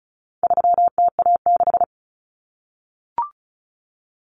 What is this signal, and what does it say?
Morse code, heard as a keyed tone, about 34 wpm, 700 Hz. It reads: 3TA6
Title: Common Cut Numbers - 4-Digit 35wpm